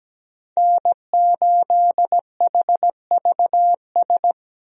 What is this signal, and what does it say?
Morse code, heard as a keyed tone, about 17 wpm, 700 Hz. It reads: N8HVS
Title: USA Call Signs - 5 Characters - 17wpm